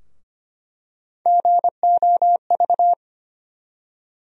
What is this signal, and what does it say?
Morse code, heard as a keyed tone, about 25 wpm, 700 Hz. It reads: GOV
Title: USA Call Sign Postfixes - 3 Letters 25wpm